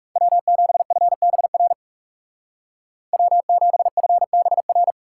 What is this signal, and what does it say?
Morse code, heard as a keyed tone, about 36 wpm, 700 Hz. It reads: W7FBR W7FBR
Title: Speed Racing - USA 1x3 Call Signs 30wpm